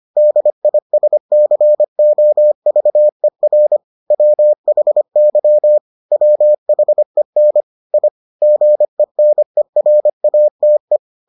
Morse code, heard as a keyed tone, about 25 wpm, 600 Hz. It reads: DISCOVER WHY WHEN I GENERATE